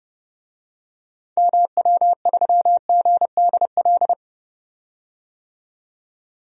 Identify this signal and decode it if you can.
Morse code, heard as a keyed tone, about 29 wpm, 700 Hz. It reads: MW3GDL
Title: DXCC-100 Call Signs - 6 Characters - 30wpm